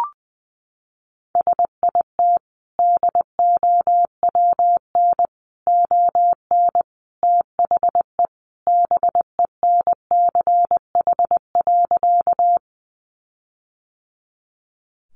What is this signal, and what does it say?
Morse code, heard as a keyed tone, about 20 wpm, 700 Hz. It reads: SIT DOWN ON THE BENCH.
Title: Sentences from Top 1400 Words 20wpm